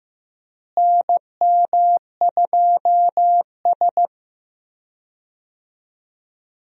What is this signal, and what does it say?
Morse code, heard as a keyed tone, about 15 wpm, 700 Hz. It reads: NM2S
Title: USA Call Signs - 4 Characters - Repeated 3x - 15wpm